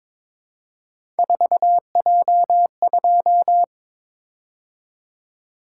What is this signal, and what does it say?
Morse code, heard as a keyed tone, about 22 wpm, 700 Hz. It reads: 4J2
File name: practice / 22wpm / Three Letter-Number Combinations - 22wpm